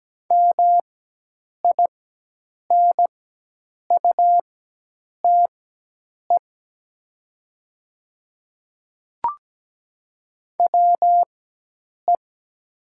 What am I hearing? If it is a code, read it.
Morse code, heard as a keyed tone, about 17 wpm, spaced as for 8 wpm, 700 Hz. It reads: MINUTE WE
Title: 4x Character Spacing - Top 500 Words - 17wpm